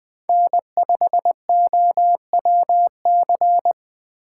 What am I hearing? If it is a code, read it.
Morse code, heard as a keyed tone, about 19 wpm, 700 Hz. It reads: N5OWC